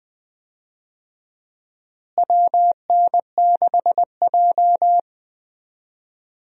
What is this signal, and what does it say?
Morse code, heard as a keyed tone, about 20 wpm, 700 Hz. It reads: WN6J
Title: USA Call Signs - 4 Characters - Repeated 2x - 20wpm